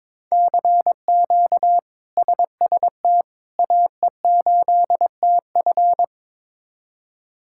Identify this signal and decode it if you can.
Morse code, heard as a keyed tone, about 22 wpm, 700 Hz. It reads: CQ SST AE8TF
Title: K1USN SST QSO Practice - 22wpm